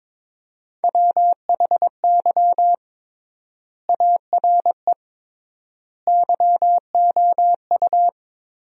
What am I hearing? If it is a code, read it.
Morse code, heard as a keyed tone, about 22 wpm, 700 Hz. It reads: WHY ARE YOU
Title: 3x Word-Spacing - Sentences from Top 200 Words - 22wpm